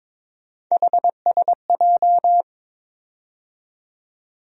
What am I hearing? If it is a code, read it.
Morse code, heard as a keyed tone, about 22 wpm, 700 Hz. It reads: HSJ
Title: USA Call Sign Postfixes - 3 Letters 22wpm